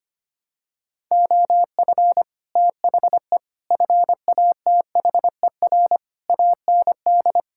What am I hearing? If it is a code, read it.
Morse code, heard as a keyed tone, about 25 wpm, 700 Hz. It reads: OF THE FATHER AND
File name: Sets of 9 Words - 25wpm